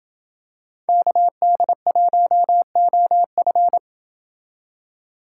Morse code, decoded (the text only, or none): KD1OF